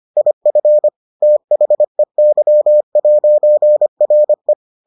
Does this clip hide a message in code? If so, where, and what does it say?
Yes: Morse code from 0.2 to 4.5 s, heard as a keyed tone, about 25 wpm, 600 Hz. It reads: IF THEY'RE